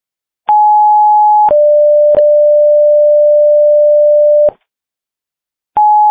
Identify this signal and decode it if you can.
Two-tone page, heard as a one-second tone, then a three-second tone, about 840 then 590 Hz. It is heard from 0.5 to 4.5 s.